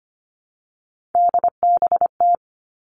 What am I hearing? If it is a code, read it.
Morse code, heard as a keyed tone, about 25 wpm, 700 Hz. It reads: DBT
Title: All Cut Numbers - 3-Digit 25wpm